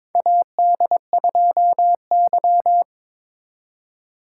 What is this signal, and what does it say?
Morse code, heard as a keyed tone, about 22 wpm, 700 Hz. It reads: AD2Y
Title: USA Call Signs - 4x Character Spacing - 22wpm